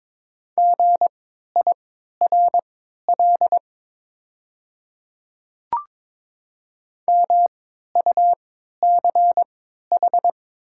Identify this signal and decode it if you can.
Morse code, heard as a keyed tone, about 22 wpm, spaced as for 13 wpm, 700 Hz. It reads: GIRL MUCH
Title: 3x Character Spacing - Top 500 Words - 22wpm